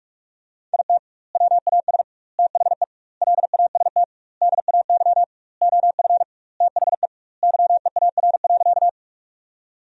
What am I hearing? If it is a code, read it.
Morse code, heard as a keyed tone, about 47 wpm, 700 Hz. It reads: IT WAS THE LAST DAY OF THE YEAR.
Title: 2x Word-Spacing - Sentences from Top 300 Words - 45wpm